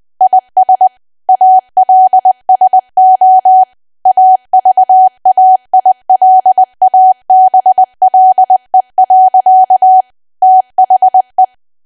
Morse code, heard as a keyed tone, about 20 wpm, 750 Hz. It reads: IS ALSO AVAILABLE. THE
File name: Archive / 20 WPM / 20WPM